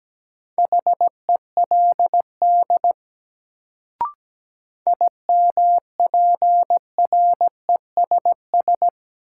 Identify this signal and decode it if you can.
Morse code, heard as a keyed tone, about 17 wpm, 700 Hz. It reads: HELD IMPRESS
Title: Sentences from Top 1600 Words - New Word Drill - 17wpm